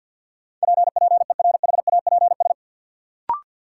Morse code, heard as a keyed tone, about 50 wpm, 700 Hz. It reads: PPERHAPS
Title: Sentences from Top 1100 Words - New Word Drill - 50wpm